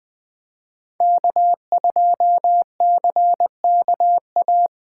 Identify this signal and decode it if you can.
Morse code, heard as a keyed tone, about 20 wpm, 700 Hz. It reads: K2CKA
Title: Speed Racing - USA Call Signs 17wpm